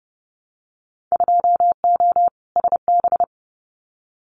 Morse code, heard as a keyed tone, about 30 wpm, 700 Hz. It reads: 2O SB
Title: Winter Field Day Exchanges 30wpm